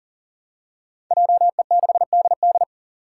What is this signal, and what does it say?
Morse code, heard as a keyed tone, about 40 wpm, 700 Hz. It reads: JE6DD
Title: Five Challenging Letter-Number Combinations 40wpm